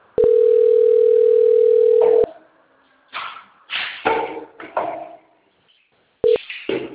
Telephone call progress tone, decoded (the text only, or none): ringback tone